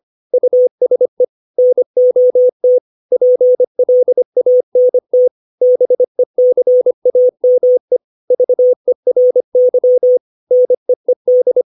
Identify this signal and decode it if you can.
Morse code, heard as a keyed tone, about 25 wpm, 500 Hz. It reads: USE NOT PLANT BECAME VERY NEED
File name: 632 words / 632 Words 25WPM